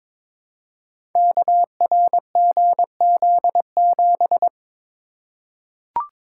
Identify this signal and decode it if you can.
Morse code, heard as a keyed tone, about 22 wpm, 700 Hz. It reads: KRGZ7
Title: Code Groups - 5 Letter:Numbers - 22wpm